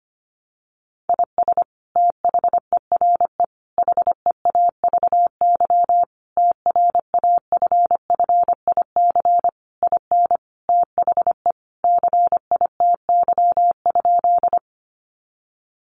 Morse code, heard as a keyed tone, about 25 wpm, 700 Hz. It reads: IS THERE HEAVY TRAFFIC IN THE CITY?
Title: Sentences from Top 1300 Words 25wpm